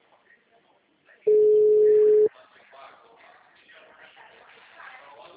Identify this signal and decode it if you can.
Ringback tone (Europe, 425 Hz), with speech.
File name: out- -2011- - - .42